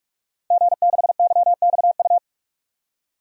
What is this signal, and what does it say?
Morse code, heard as a keyed tone, about 46 wpm, 700 Hz. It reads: G6YXU
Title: DXCC-100 Call Signs - 5 Characters - 45wpm